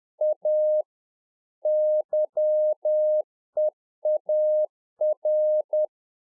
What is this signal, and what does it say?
Morse code, heard as a keyed tone, about 10 wpm, 610 Hz. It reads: A YEAR
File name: 2011-04-mp3 / A SMALL TRANCEIVER 1 10WPM